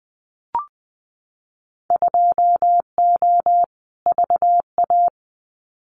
Morse code, heard as a keyed tone, about 19 wpm, 700 Hz. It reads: 2O VA
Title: Winter Field Day Exchanges 20wpm